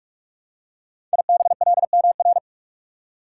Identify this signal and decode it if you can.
Morse code, heard as a keyed tone, about 44 wpm, 690 Hz. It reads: IBLMR
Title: Code Groups - 5 Letters - 45wpm